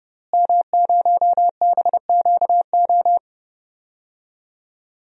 Morse code, heard as a keyed tone, about 30 wpm, 700 Hz. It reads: M0BQO
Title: DXCC-100 Call Signs - 5 Characters - Repeated 2x - 30wpm